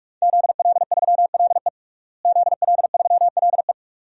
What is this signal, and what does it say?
Morse code, heard as a keyed tone, about 41 wpm, 700 Hz. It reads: ZL3LE ZL3LE
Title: DXCC-100 Call Signs - 5 Characters - Repeated 2x - 45wpm